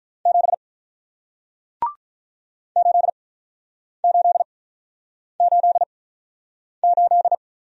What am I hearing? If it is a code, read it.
Morse code, heard as a keyed tone, about 48 wpm, spaced as for 9 wpm, 700 Hz. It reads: 7 8888